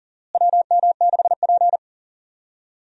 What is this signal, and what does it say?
Morse code, heard as a keyed tone, about 40 wpm, 700 Hz. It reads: WM6P
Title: USA Call Signs - 4 Characters - Repeated 3x - 40wpm